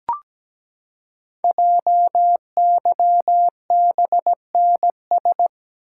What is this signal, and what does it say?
Morse code, heard as a keyed tone, about 17 wpm, 700 Hz. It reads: JYBNS
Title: Code Groups - 5 Letters - 17wpm